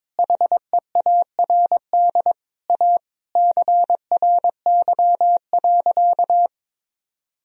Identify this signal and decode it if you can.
Morse code, heard as a keyed tone, about 22 wpm, 700 Hz. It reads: HEARD A CRY.
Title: Sentences from Top 1000 Words - Review Part 2 - 22wpm